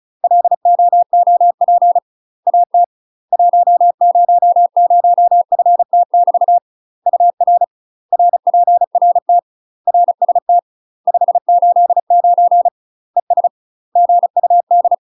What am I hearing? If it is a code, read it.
Morse code, heard as a keyed tone, about 37 wpm, 700 Hz. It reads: LOOP AT 100FT<BT> UR RPRT RST 589 ES GUD
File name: UK MASTER QSOs TRAINING / 35WPM / UK MASTER QSOs 0001